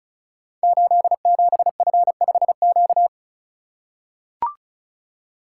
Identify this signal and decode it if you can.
Morse code, heard as a keyed tone, about 36 wpm, 700 Hz. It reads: 87F5Q
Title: Code Groups - 5 Letter:Numbers - 35wpm